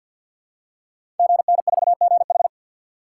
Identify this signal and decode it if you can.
Morse code, heard as a keyed tone, about 52 wpm, 690 Hz. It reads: GN4GH
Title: Code Groups - 5 Letter:Numbers - 50wpm